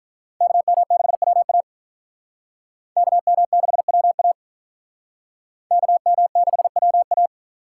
Morse code, heard as a keyed tone, about 46 wpm, 700 Hz. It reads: KM6WA KM6WA KM6WA